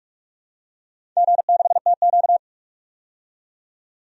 Morse code, heard as a keyed tone, about 45 wpm, 700 Hz. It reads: G6TQ
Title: DXCC-100 Call Signs - 4 Characters - 45wpm